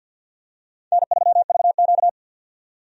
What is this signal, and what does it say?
Morse code, heard as a keyed tone, about 53 wpm, 700 Hz. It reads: N3VQ
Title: USA Call Signs - 4 Characters - 50wpm